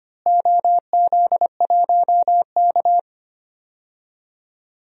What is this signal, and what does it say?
Morse code, heard as a keyed tone, about 25 wpm, 700 Hz. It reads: OZ1K
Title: DXCC-100 Call Signs - 4 Characters - Repeated 3x - 25wpm